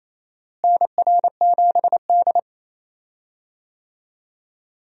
Morse code, heard as a keyed tone, about 28 wpm, 700 Hz. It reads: NR7D
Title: USA Call Signs - 4 Characters - Repeated 3x - 28wpm